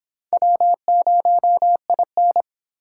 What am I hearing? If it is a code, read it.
Morse code, heard as a keyed tone, about 26 wpm, 700 Hz. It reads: W0IN